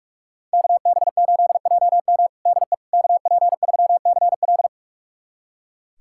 Morse code, heard as a keyed tone, about 46 wpm, 700 Hz. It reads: KB8JM DE KP3CL
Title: USA Callsign DE Callsign - 5 Characters - 45wpm